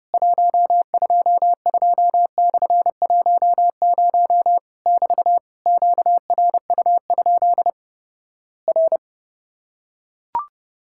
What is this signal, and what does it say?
Morse code, heard as a keyed tone, about 31 wpm, 700 Hz. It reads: 122/10 <BT> QRU? R
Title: Worked All Europe DX Contest - QTC Exchange - 30wpm